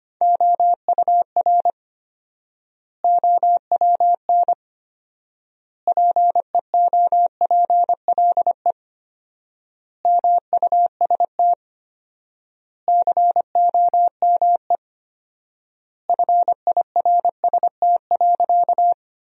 Morse code, heard as a keyed tone, about 25 wpm, 700 Hz. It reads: OUR OWN PEOPLE MUST COME FIRST.
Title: 4x Word-Spacing - Sentences from Top 300 Words - 25wpm